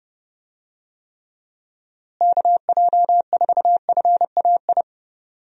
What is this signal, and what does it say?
Morse code, heard as a keyed tone, about 32 wpm, 700 Hz. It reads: KJ4FAI
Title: USA Call Signs - 6 Characters - Repeated 3x - 30wpm